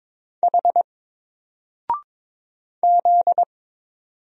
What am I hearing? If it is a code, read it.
Morse code, heard as a keyed tone, about 22 wpm, 700 Hz. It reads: H Z
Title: Single Letter-Number Alt 22wpm